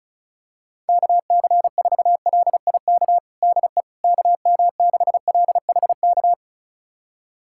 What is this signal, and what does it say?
Morse code, heard as a keyed tone, about 36 wpm, 700 Hz. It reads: KC4LIK DE KM6LHK
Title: USA Callsign DE Callsign - 6 Characters - 35wpm